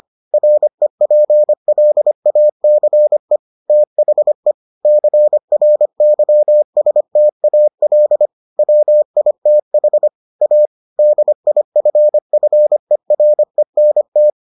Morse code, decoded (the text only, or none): REPLACE THE CRYSTAL WITH A DIFFERENT